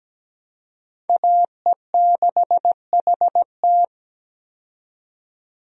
Morse code, heard as a keyed tone, about 17 wpm, 700 Hz. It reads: AE6HT
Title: USA Call Signs - 5 Characters - 17wpm